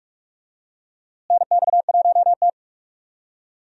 Morse code, heard as a keyed tone, about 42 wpm, 700 Hz. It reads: NX1T
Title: USA Call Signs - 4 Characters - Repeated 3x - 45wpm